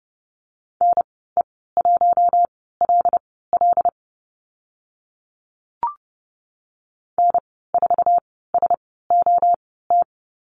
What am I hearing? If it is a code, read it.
Morse code, heard as a keyed tone, about 30 wpm, spaced as for 17 wpm, 700 Hz. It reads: NE1LL N4SOT
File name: Lesson 14g - Number 2 - Call Signs 30wpm